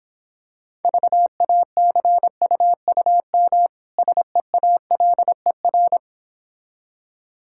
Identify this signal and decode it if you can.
Morse code, heard as a keyed tone, about 26 wpm, 690 Hz. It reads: VACUUM SEALER